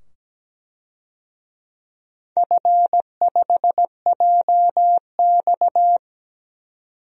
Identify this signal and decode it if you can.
Morse code, heard as a keyed tone, about 17 wpm, 700 Hz. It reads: F5JX
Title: DXCC-100 Call Signs - 4 Characters - 17wpm